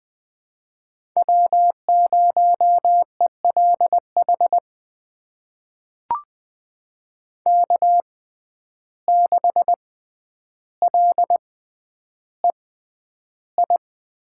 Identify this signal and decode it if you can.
Morse code, heard as a keyed tone, about 20 wpm, spaced as for 10 wpm, 700 Hz. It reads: W0ELH K6LEI
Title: USA Call Signs - 5 Characters - 6x Character Spacing - 20wpm